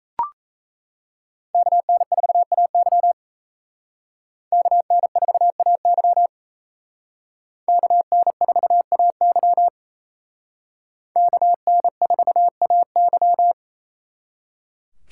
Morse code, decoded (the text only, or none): KN4AY KN4AY KN4AY KN4AY